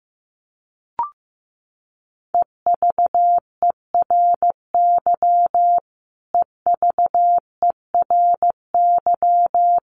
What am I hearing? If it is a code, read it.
Morse code, heard as a keyed tone, about 15 wpm, 700 Hz. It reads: EVERY EVERY
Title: Top 200 Words Repeated 15wpm